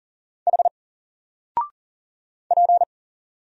Morse code, heard as a keyed tone, about 40 wpm, 700 Hz. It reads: H P
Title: Single Letter-Number Alt 40wpm